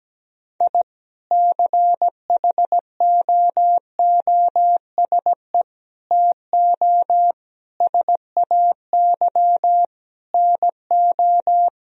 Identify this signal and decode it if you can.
Morse code, heard as a keyed tone, about 17 wpm, 700 Hz. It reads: I CHOOSE TO SAY NO
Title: Sentences from Top 1000 Words - Review Part 4 - 17wpm